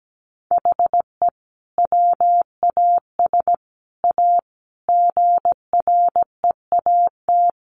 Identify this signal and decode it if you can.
Morse code, heard as a keyed tone, about 17 wpm, 700 Hz. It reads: HE WAS A GREAT
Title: Sentences from Top 1200 Words 17wpm